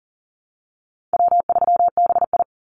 Morse code, heard as a keyed tone, about 40 wpm, 700 Hz. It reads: W3BI
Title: Easy Call Signs - 40wpm